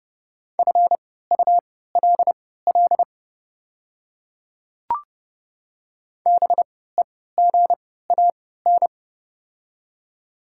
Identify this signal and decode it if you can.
Morse code, heard as a keyed tone, about 30 wpm, spaced as for 17 wpm, 690 Hz. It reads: FULL BEGAN
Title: 3x Character Spacing - Top 500 Words - 30wpm